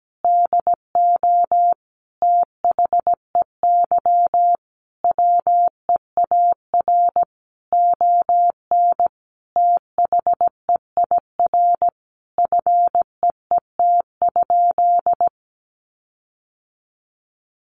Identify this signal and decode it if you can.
Morse code, heard as a keyed tone, about 17 wpm, 700 Hz. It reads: DO THEY WEAR ON THEIR FEET?
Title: Sentences from Top 1500 Words 17wpm